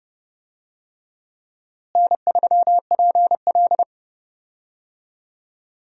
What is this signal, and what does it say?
Morse code, heard as a keyed tone, about 30 wpm, 700 Hz. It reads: N3PL
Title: USA Call Signs - 4 Characters - Repeated 3x - 30wpm